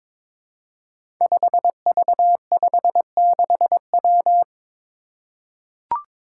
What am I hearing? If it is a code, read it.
Morse code, heard as a keyed tone, about 23 wpm, 700 Hz. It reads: 5V56W
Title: Five Challenging Letter-Number Combinations 22wpm